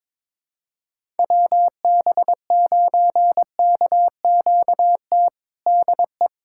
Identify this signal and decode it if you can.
Morse code, heard as a keyed tone, about 21 wpm, 700 Hz. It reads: WB9KQT DE
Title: USA Callsign DE Callsign - 6 Characters - 22wpm